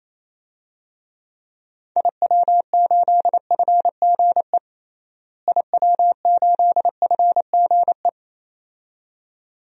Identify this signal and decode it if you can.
Morse code, heard as a keyed tone, about 28 wpm, 700 Hz. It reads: IW8FGE IW8FGE
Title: DXCC-100 Call Signs - 6 Characters - Repeated 2x - 28wpm